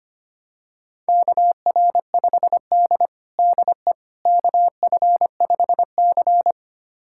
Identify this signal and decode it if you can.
Morse code, heard as a keyed tone, about 25 wpm, 700 Hz. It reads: KR5D DE KF5C
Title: USA Callsign DE Callsign - 4 Characters - 25wpm